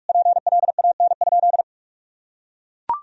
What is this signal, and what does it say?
Morse code, heard as a keyed tone, about 41 wpm, 700 Hz. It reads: PLAN?
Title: Sentences from Top 1000 Words - Review Part 3 - 45wpm